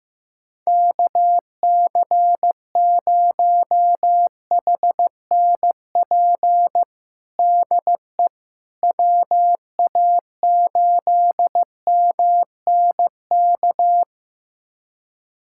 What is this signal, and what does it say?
Morse code, heard as a keyed tone, about 15 wpm, 700 Hz. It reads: KC0HNP DE WA8MNK
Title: USA Callsign DE Callsign 15wpm